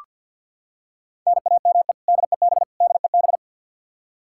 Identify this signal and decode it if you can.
Morse code, heard as a keyed tone, about 48 wpm, 700 Hz. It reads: NAME DEB DEB
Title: Practice QSO - Exchange 1 - Beginning - 50wpm